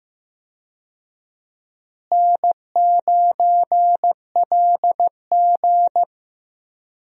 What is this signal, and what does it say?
Morse code, heard as a keyed tone, about 16 wpm, 700 Hz. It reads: N9LG
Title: USA Call Signs - 4 Characters - Repeated 2x - 15wpm